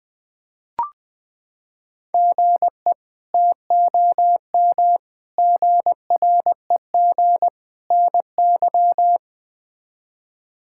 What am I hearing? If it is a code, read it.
Morse code, heard as a keyed tone, about 20 wpm, 700 Hz. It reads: GE TOM GREG NY
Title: K1USN SST QSO Practice - 20wpm